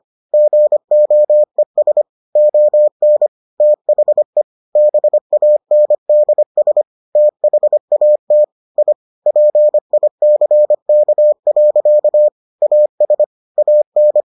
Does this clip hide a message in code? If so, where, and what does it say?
Yes: Morse code from 0.3 to 14.2 s, heard as a keyed tone, about 25 wpm, 600 Hz. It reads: GOES ON THE BANDS THAT I PICK. AS AN